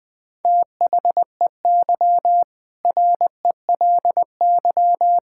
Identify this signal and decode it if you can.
Morse code, heard as a keyed tone, about 20 wpm, 700 Hz. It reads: THEY RELY
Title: Sentences with 2-Word Phrases - 20wpm